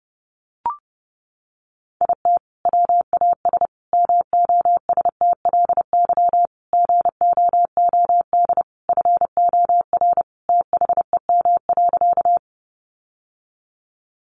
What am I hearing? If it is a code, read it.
Morse code, heard as a keyed tone, about 28 wpm, 700 Hz. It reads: IT WAS MOSTLY GOOD FOR THEM.
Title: Sentences from Top 1300 Words 30wpm